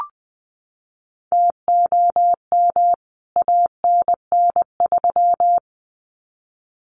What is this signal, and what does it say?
Morse code, heard as a keyed tone, about 20 wpm, 700 Hz. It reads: TOM ANN3
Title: CWT Exchange 20wpm